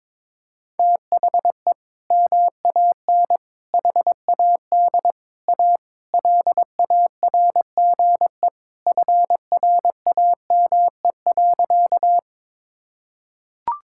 Morse code, heard as a keyed tone, about 22 wpm, 700 Hz. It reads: THE MAN HAD A LARGE FRAME.
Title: Sentences from Top 1000 Words - Review Part 6 - 22wpm